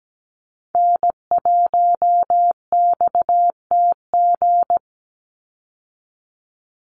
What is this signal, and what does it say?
Morse code, heard as a keyed tone, about 17 wpm, 700 Hz. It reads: N1XTG
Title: USA Call Signs 17wpm